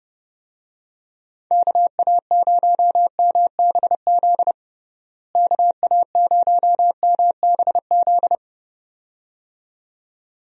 Morse code, heard as a keyed tone, about 32 wpm, 700 Hz. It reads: KA0MBZ KA0MBZ
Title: USA Call Signs - 6 Characters - Repeated 2x - 30wpm